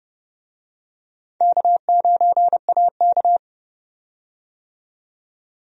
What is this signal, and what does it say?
Morse code, heard as a keyed tone, about 30 wpm, 700 Hz. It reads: K9AK
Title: USA Call Signs - 4 Characters - 30wpm